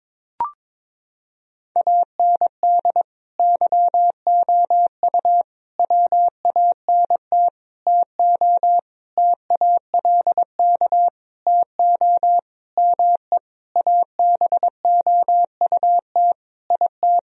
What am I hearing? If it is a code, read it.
Morse code, heard as a keyed tone, about 22 wpm, 700 Hz. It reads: AND YOU WANT TO TALK TO ME ABOUT IT
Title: Sentences from Top 1000 Words - Review Part 4 - 22wpm